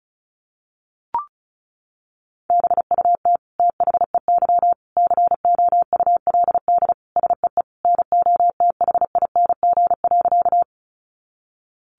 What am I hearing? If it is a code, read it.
Morse code, heard as a keyed tone, about 34 wpm, 700 Hz. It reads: BUT THEY COULD SEE NOTHING.
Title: Sentences from Top 500 Words 35wpm